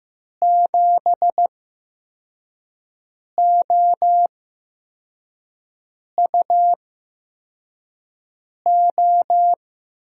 Morse code, heard as a keyed tone, about 15 wpm, spaced as for 4 wpm, 700 Hz. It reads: 7OUO
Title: USA Call Signs - 6 Characters - 8x Character Spacing - 15wpm